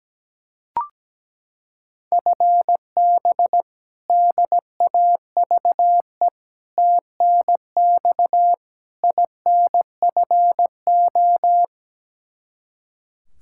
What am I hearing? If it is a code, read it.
Morse code, heard as a keyed tone, about 17 wpm, 700 Hz. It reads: FB DAVE TNX INFO
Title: K1USN SST QSO Practice 17wpm